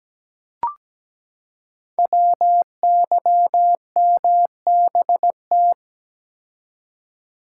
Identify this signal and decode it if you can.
Morse code, heard as a keyed tone, about 17 wpm, 700 Hz. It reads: WYMBT
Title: Code Groups - 5 Letters - 17wpm